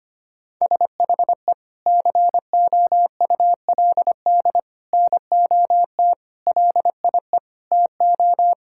Morse code, heard as a keyed tone, about 25 wpm, 700 Hz. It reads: SHE COULD NOT LIE TO
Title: Sentences from Top 1600 Words 25wpm